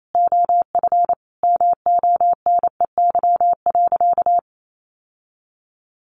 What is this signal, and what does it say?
Morse code, heard as a keyed tone, about 28 wpm, 700 Hz. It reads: OF MONEY.
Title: Sentences from Top 1400 Words 28wpm